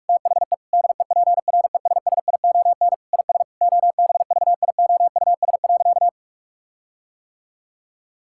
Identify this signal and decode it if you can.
Morse code, heard as a keyed tone, about 41 wpm, 700 Hz. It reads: THE DEPRESSION IS OBVIOUS.
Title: Sentences from Top 1600 Words 45wpm